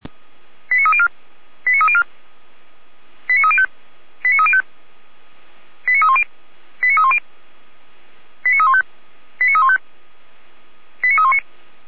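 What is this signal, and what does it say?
Five-tone selective call (ZVEI), heard as quick runs of five tones, tones of about 70 ms. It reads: 89304 89304 89305 89305 89310 89310 89315 89315 89319
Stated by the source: ZVEI five-tone call 89286, received on SDR.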